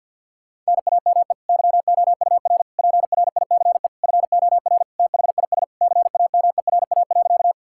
Morse code, heard as a keyed tone, about 50 wpm, 700 Hz. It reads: NAME YOUR PRICE FOR THIS CAMERA.
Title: Sentences from Top 1000 Words - Review Part 5 - 50wpm